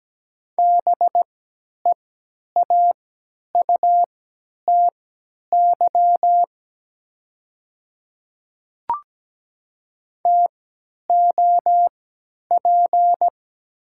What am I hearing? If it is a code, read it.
Morse code, heard as a keyed tone, about 17 wpm, spaced as for 10 wpm, 700 Hz. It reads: BEAUTY TOP
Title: 3x Character Spacing - Top 500 Words - 17wpm